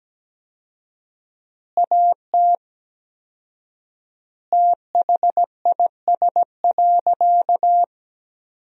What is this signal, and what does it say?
Morse code, heard as a keyed tone, about 17 wpm, 700 Hz. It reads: AT THIS.
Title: 4x Word-Spacing - Sentences from Top 300 Words - 17wpm